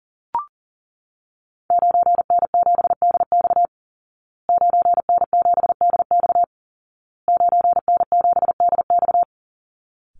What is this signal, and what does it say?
Morse code, heard as a keyed tone, about 40 wpm, 700 Hz. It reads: 9N7DX 9N7DX 9N7DX
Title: Call Signs Repeated 2x - 40wpm